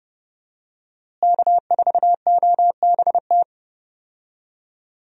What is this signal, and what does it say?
Morse code, heard as a keyed tone, about 30 wpm, 700 Hz. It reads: K4OBT
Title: USA Call Signs - 5 Characters - Repeated 3x - 30wpm